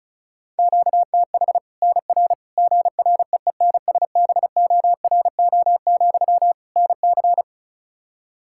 Morse code, heard as a keyed tone, about 36 wpm, 700 Hz. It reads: QTH NR GREENSBORO, NC
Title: Practice QSO - Exchange 1 - Beginning - 35wpm